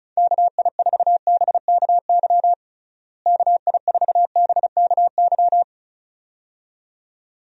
Morse code, heard as a keyed tone, about 36 wpm, 700 Hz. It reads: KI4BKY KI4BKY
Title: USA Call Signs - 6 Characters - Repeated 2x - 35wpm